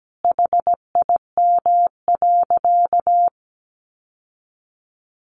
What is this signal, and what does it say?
Morse code, heard as a keyed tone, about 17 wpm, 700 Hz. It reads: HIM.
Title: Sentences from Top 1200 Words 17wpm